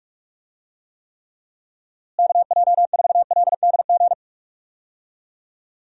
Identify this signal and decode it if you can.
Morse code, heard as a keyed tone, about 46 wpm, 700 Hz. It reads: KJ4LDG
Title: USA Call Signs - 6 Characters - Repeated 3x - 45wpm